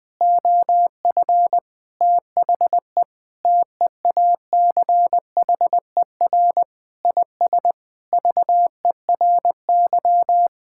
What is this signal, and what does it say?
Morse code, heard as a keyed tone, about 20 wpm, 700 Hz. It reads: OF THE TEACHER IS VERY
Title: Sentences with 4-Word Phrases - Spoken Beginning - 20wpm